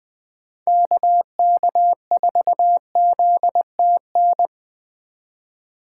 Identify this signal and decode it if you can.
Morse code, heard as a keyed tone, about 20 wpm, 700 Hz. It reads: KK4ZTN
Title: USA Call Signs - 6 Characters - 2x Character Spacing - 20wpm